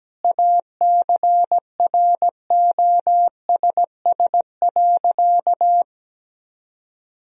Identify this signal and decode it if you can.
Morse code, heard as a keyed tone, about 17 wpm, 700 Hz. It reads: ACROSS.
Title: Sentences from Top 1000 Words - Review Part 3 - 17wpm